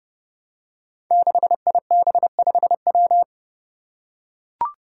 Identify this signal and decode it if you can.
Morse code, heard as a keyed tone, about 30 wpm, 700 Hz. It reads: 6IB5W
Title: Five Challenging Letter-Number Combinations 30wpm